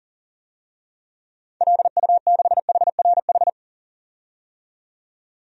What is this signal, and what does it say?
Morse code, heard as a keyed tone, about 40 wpm, 700 Hz. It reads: LU6HRH
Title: DXCC-100 Call Signs - 6 Characters - 40wpm